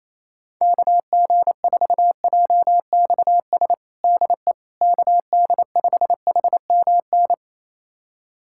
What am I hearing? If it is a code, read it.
Morse code, heard as a keyed tone, about 28 wpm, 700 Hz. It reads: KG4JXS DE KD5HMN
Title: USA Callsign DE Callsign - 6 Characters - 28wpm